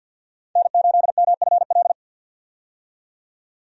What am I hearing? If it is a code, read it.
Morse code, heard as a keyed tone, about 50 wpm, 700 Hz. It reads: N8MFL
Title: USA Call Signs - 5 Characters - 50wpm